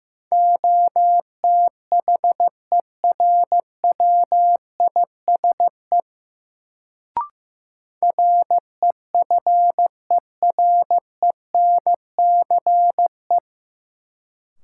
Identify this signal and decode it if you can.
Morse code, heard as a keyed tone, about 15 wpm, 700 Hz. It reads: OTHERWISE REFERENCE
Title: Sentences from Top 1200 Words - New Word Drill Alt - 15wpm